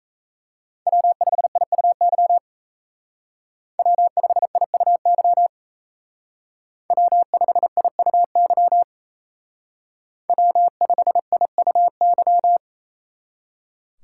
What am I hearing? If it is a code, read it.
Morse code, heard as a keyed tone, about 36 wpm, 700 Hz. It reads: W5IUY W5IUY W5IUY W5IUY